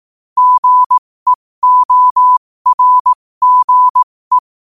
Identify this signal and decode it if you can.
Morse code, heard as a keyed tone, about 18 wpm, spaced as for 15 wpm, 1000 Hz. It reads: GEORGE
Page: New quotes every day in morse code at 15 Words per minute.